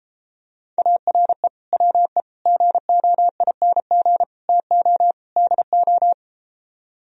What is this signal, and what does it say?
Morse code, heard as a keyed tone, about 34 wpm, 700 Hz. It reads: ARE WE GOING TO DO